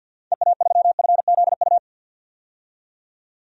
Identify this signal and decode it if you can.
Morse code, heard as a keyed tone, about 50 wpm, 700 Hz. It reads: EA3FZU
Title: DXCC-100 Call Signs - 6 Characters - 50wpm